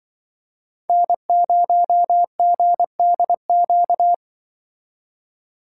Morse code, heard as a keyed tone, about 25 wpm, 700 Hz. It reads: N0GDQ
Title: Speed Racing - USA 1x3 Call Signs 20wpm